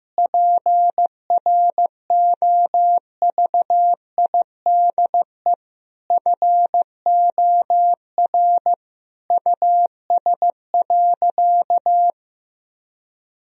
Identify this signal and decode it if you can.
Morse code, heard as a keyed tone, about 16 wpm, 700 Hz. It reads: PROVIDE FOR US.
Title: Sentences from Top 1000 Words - Review Part 3 - 15wpm